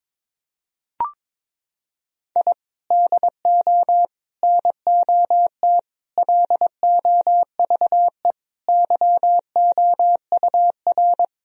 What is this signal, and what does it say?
Morse code, heard as a keyed tone, about 22 wpm, 700 Hz. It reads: I DO NOT LOVE YOUR
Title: Sentences from Top 1000 Words - Review Part 4 - 22wpm